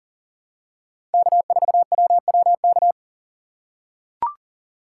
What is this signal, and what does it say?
Morse code, heard as a keyed tone, about 40 wpm, 700 Hz. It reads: K4WWK
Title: USA Call Signs - 5 Characters - 4x Character Spacing - 40wpm